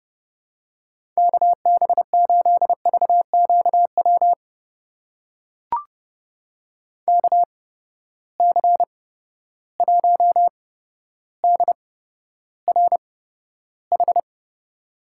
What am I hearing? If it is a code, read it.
Morse code, heard as a keyed tone, about 32 wpm, spaced as for 13 wpm, 700 Hz. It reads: KB8VQW KC1DRH
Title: USA Call Signs - 8x Character Spacing - 30wpm